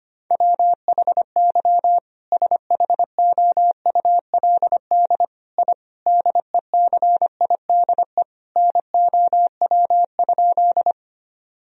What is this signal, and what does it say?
Morse code, heard as a keyed tone, about 25 wpm, 700 Hz. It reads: WHY SHOULD I DECIDE NOW?
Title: Sentences from Top 1000 Words - Review Part 3 - 25wpm